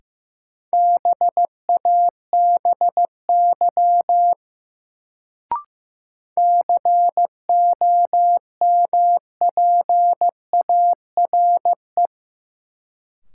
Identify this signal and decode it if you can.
Morse code, heard as a keyed tone, about 15 wpm, 700 Hz. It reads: BABY COMPARE
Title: Sentences from Top 1300 Words - New Word Drill Alt - 15wpm